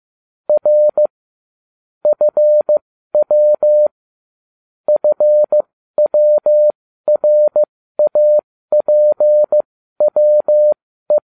Morse code, heard as a keyed tone, about 15 wpm, spaced as for 12 wpm, 600 Hz. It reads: R FW FWRAPWE
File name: cw / ubacw20 3